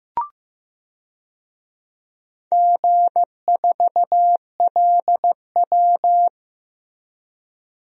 Morse code, heard as a keyed tone, about 15 wpm, 700 Hz. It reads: G4LW
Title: DXCC-100 Call Signs - 4 Characters - Repeated 2x - 15wpm